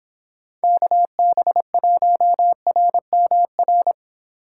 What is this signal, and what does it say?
Morse code, heard as a keyed tone, about 24 wpm, 700 Hz. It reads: KB1RMR